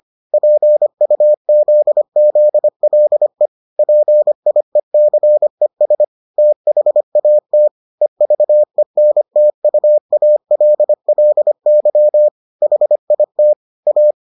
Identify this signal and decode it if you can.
Morse code, heard as a keyed tone, about 25 wpm, 600 Hz. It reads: PUZZLE PIECES THAT EVENTUALLY HIT A